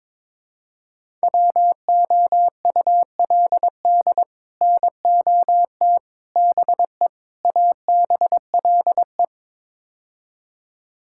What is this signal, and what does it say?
Morse code, heard as a keyed tone, about 22 wpm, 700 Hz. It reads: WOULD NOT BE ABLE
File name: practice / 22wpm / Build A Phrase - 5-Word Phrases - 22wpm